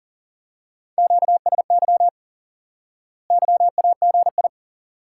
Morse code, heard as a keyed tone, about 38 wpm, 700 Hz. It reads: QSY YAGI
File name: course / instant-qso-element-course-3 / IQE Course 3 - Lesson 89 - YAGI - Two Elements - 40wpm